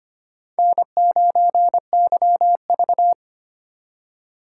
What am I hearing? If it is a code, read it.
Morse code, heard as a keyed tone, about 25 wpm, 700 Hz. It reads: N9YV
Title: USA Call Signs - 2x Character Spacing - 25wpm